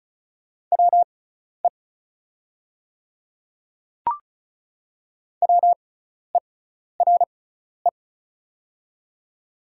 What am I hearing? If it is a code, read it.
Morse code, heard as a keyed tone, about 36 wpm, spaced as for 12 wpm, 700 Hz. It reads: WE WERE
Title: 6x Character Spacing - Top 100 Words - 35wpm